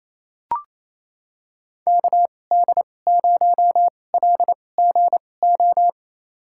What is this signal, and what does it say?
Morse code, heard as a keyed tone, about 28 wpm, spaced as for 20 wpm, 700 Hz. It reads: KD0LGO
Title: USA Call Signs - 6 Characters - 2x Character Spacing - 28wpm